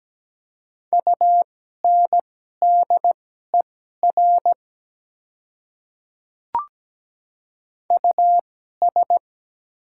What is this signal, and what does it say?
Morse code, heard as a keyed tone, about 17 wpm, spaced as for 12 wpm, 700 Hz. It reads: UNDER US
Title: 2x Character Spacing - Top 200 Words - 17wpm